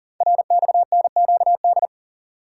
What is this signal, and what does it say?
Morse code, heard as a keyed tone, about 42 wpm, 690 Hz. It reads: RXNQD
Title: Code Groups - 5 Letters - 40wpm